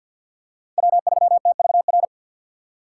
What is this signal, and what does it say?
Morse code, heard as a keyed tone, about 53 wpm, 700 Hz. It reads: W3TVR
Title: USA Call Signs - 2x Character Spacing - 50wpm